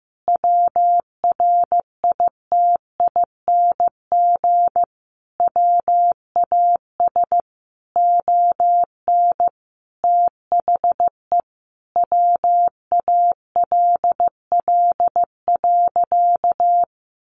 Morse code, heard as a keyed tone, about 15 wpm, 700 Hz. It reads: WRITING WAS ON THE WALL.
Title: Sentences from Top 1100 Words 15wpm